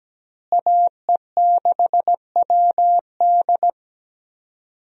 Morse code, heard as a keyed tone, about 17 wpm, 700 Hz. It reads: AE6WD
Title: USA Call Signs - 5 Characters - Repeated 3x - 17wpm